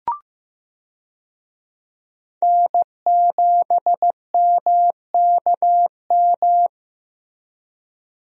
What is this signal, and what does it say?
Morse code, heard as a keyed tone, about 14 wpm, 700 Hz. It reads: N7MKM
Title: USA Call Signs - 5 Characters - Repeated 2x - 15wpm